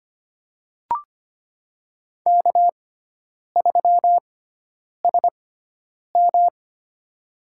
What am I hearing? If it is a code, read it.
Morse code, heard as a keyed tone, about 25 wpm, spaced as for 9 wpm, 700 Hz. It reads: K3SM
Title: USA Call Signs - 4 Characters - 6x Character Spacing - 25wpm